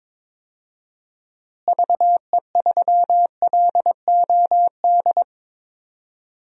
Morse code, heard as a keyed tone, about 22 wpm, 700 Hz. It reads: VE3LOD
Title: DXCC-100 Call Signs - 6 Characters - Repeated 2x - 22wpm